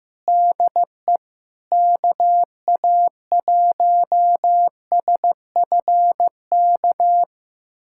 Morse code, heard as a keyed tone, about 16 wpm, 700 Hz. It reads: DE KA1SFK
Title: USA Callsign DE Callsign 15wpm